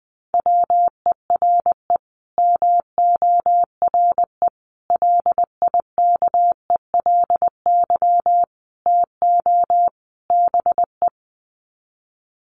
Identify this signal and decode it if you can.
Morse code, heard as a keyed tone, about 19 wpm, 700 Hz. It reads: WERE MORE LIKELY TO BE
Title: Sets of 5 Words Encore2 20wpm